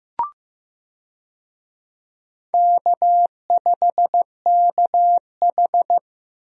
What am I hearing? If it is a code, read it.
Morse code, heard as a keyed tone, about 15 wpm, 700 Hz. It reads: K5KH
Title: USA Call Signs - 4 Characters - Repeated 2x - 15wpm